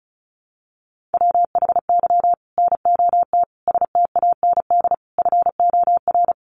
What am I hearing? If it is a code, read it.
Morse code, heard as a keyed tone, about 34 wpm, 700 Hz. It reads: WHY NOT STAND FOR
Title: Sentences from Top 1500 Words 35wpm